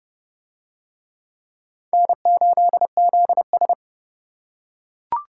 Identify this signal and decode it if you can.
Morse code, heard as a keyed tone, about 26 wpm, 700 Hz. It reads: N8ZS